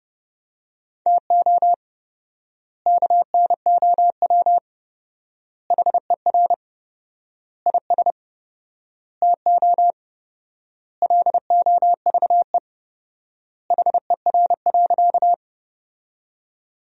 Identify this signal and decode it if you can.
Morse code, heard as a keyed tone, about 30 wpm, 700 Hz. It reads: TO KNOW HER IS TO LOVE HER.
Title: 4x Word-Spacing - Sentences from Top 400 Words - 30wpm